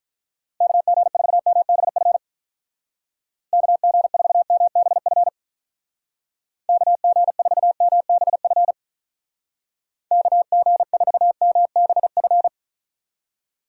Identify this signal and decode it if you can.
Morse code, heard as a keyed tone, about 45 wpm, 700 Hz. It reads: KG4MBF KG4MBF KG4MBF KG4MBF